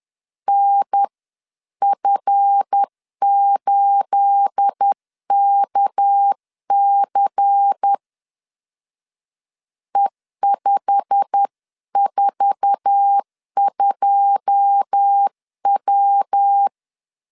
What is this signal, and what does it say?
Morse code, heard as a keyed tone, about 10 wpm, 800 Hz. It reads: NF8KC E542W
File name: LEZIONI / 9 28 10wpm 2842